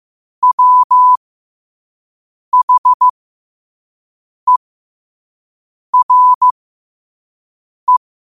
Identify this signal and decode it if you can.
Morse code, heard as a keyed tone, about 15 wpm, spaced as for 5 wpm, 1000 Hz. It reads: WHERE
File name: qotdfiles / 5wpm